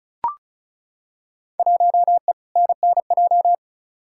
Morse code, heard as a keyed tone, about 34 wpm, 700 Hz. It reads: 1E NNJ
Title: Field Day Exchanges - 35wpm